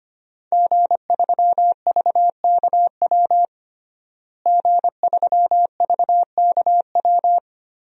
Morse code, heard as a keyed tone, about 25 wpm, 700 Hz. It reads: G3VKW G3VKW
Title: DXCC-100 Call Signs - 5 Characters - Repeated 2x - 25wpm